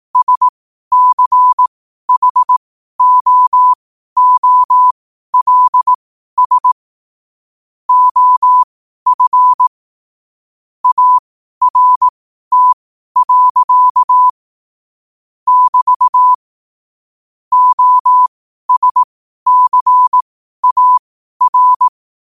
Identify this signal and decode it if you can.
Morse code, heard as a keyed tone, about 18 wpm, spaced as for 12 wpm, 1000 Hz. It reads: SCHOOLS OF ART. <BT> OSCAR
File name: qotdfiles / 12wpm